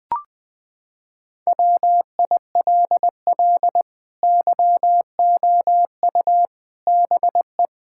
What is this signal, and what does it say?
Morse code, heard as a keyed tone, about 20 wpm, 700 Hz. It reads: WILL YOU BE
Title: Sentences from Top 1600 Words 20wpm